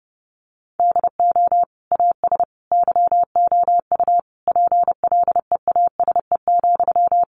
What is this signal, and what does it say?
Morse code, heard as a keyed tone, about 30 wpm, 700 Hz. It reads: DO AS YOU PLEASE,
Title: Sentences from Top 1500 Words 30wpm